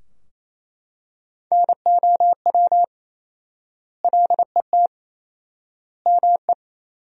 Morse code, heard as a keyed tone, about 27 wpm, 700 Hz. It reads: NOW LET ME
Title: 4x Word-Spacing - Sentences from Top 300 Words - 28wpm